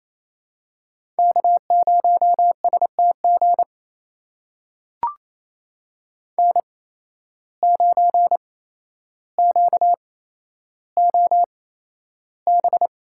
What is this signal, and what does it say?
Morse code, heard as a keyed tone, about 28 wpm, spaced as for 12 wpm, 700 Hz. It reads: K0STG N9QOB
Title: USA Call Signs - 5 Characters - 8x Character Spacing - 28wpm